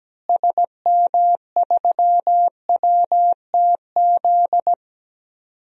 Morse code, heard as a keyed tone, about 17 wpm, 700 Hz. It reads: SM3WTZ
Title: DXCC-100 Call Signs - 6 Characters - Repeated 2x - 17wpm